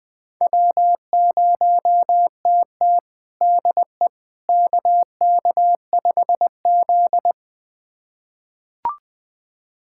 Morse code, heard as a keyed tone, about 21 wpm, 700 Hz. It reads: W0TT DE KK5Z
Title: USA Callsign DE Callsign - 4 Characters - 20wpm